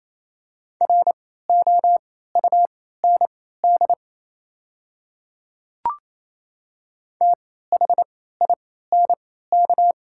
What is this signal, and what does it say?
Morse code, heard as a keyed tone, about 28 wpm, spaced as for 16 wpm, 700 Hz. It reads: ROUND THINK
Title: 3x Character Spacing - Top 200 Words - 28wpm